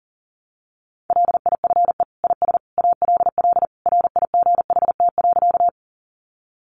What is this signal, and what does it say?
Morse code, heard as a keyed tone, about 40 wpm, 700 Hz. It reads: LIFE IS ALL RIGHT.
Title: Sentences from Top 300 Words 40wpm